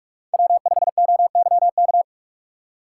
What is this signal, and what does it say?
Morse code, heard as a keyed tone, about 44 wpm, 690 Hz. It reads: W5OYK
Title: USA Call Signs - 4x Character Spacing - 45wpm